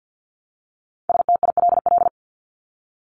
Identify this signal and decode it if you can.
Morse code, heard as a keyed tone, about 50 wpm, 700 Hz. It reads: STILL
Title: Top 500 Words 50wpm